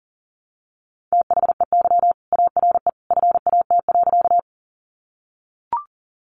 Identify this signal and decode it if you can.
Morse code, heard as a keyed tone, about 40 wpm, 700 Hz. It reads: THEY ARE FAT.
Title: Sentences from Top 600 Words 40wpm